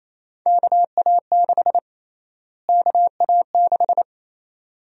Morse code, heard as a keyed tone, about 28 wpm, 700 Hz. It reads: KA6 KA6
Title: USA Call Sign Prefixes - Repeated 2x - 28wpm